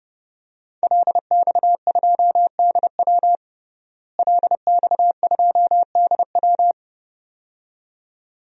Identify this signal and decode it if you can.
Morse code, heard as a keyed tone, about 30 wpm, 700 Hz. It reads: LX2DW LX2DW
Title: DXCC-100 Call Signs - 5 Characters - Repeated 2x - 30wpm